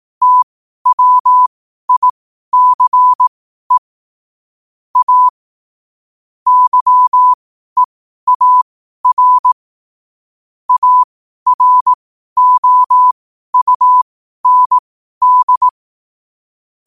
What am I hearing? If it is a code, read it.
Morse code, heard as a keyed tone, about 18 wpm, spaced as for 12 wpm, 1000 Hz. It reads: TWICE A YEAR AROUND
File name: qotdfiles / 12wpm